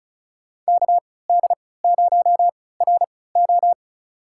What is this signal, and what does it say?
Morse code, heard as a keyed tone, about 36 wpm, spaced as for 20 wpm, 700 Hz. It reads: KD0RO
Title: USA Call Signs - 5 Characters - 3x Character Spacing - 35wpm